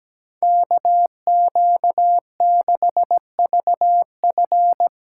Morse code, heard as a keyed tone, about 17 wpm, 700 Hz. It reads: KQ6VF
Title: USA Callsign DE Callsign - 5 Characters - 17wpm